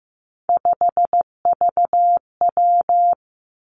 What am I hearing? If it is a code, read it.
Morse code, heard as a keyed tone, about 15 wpm, 700 Hz. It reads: 5VW
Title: Three Challenging Letter-Number Combinations 15wpm